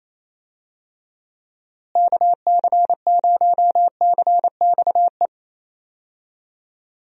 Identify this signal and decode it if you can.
Morse code, heard as a keyed tone, about 28 wpm, 700 Hz. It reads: KC0CXE
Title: USA Call Signs - 6 Characters - Repeated 2x - 28wpm